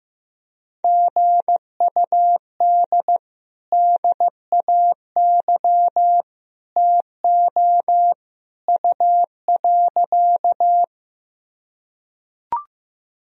Morse code, heard as a keyed tone, about 16 wpm, 700 Hz. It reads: GUD DAY TO U.
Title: Sentences from Top 200 Words and Abbreviations 15wpm